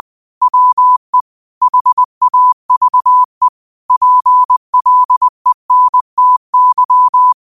Morse code, heard as a keyed tone, about 20 wpm, 1000 Hz. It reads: WE HAVE PLENTY
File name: qotdfiles / 20wpm